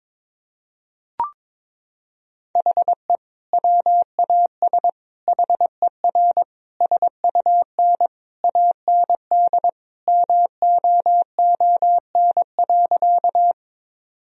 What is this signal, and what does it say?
Morse code, heard as a keyed tone, about 22 wpm, 690 Hz. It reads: HE WAS HER SUN AND MOON.
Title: Sentences from Top 1000 Words - Review Part 3 - 22wpm